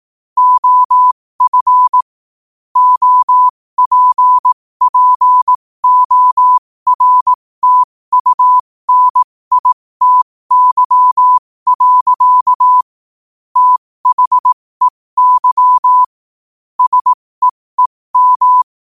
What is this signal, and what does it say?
Morse code, heard as a keyed tone, about 18 wpm, spaced as for 15 wpm, 1000 Hz. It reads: OF OPPORTUNITY. THEY SEEM
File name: qotdfiles / 15wpm